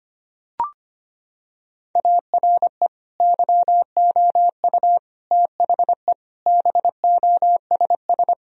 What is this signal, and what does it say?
Morse code, heard as a keyed tone, about 25 wpm, 700 Hz. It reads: ARE YOU THE BOSS
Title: Sentences from Top 1000 Words - Review Part 4 - 25wpm